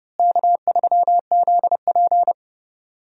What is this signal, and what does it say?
Morse code, heard as a keyed tone, about 30 wpm, 700 Hz. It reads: K3ZP
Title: USA Call Signs - 4 Characters - Repeated 3x - 30wpm